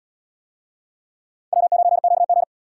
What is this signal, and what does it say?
Morse code, heard as a keyed tone, about 70 wpm, 700 Hz. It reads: W9QG